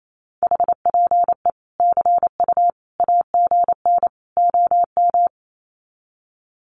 Morse code, heard as a keyed tone, about 28 wpm, 700 Hz. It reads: HPE CU AGN OM
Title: Common QSO Phrases 28wpm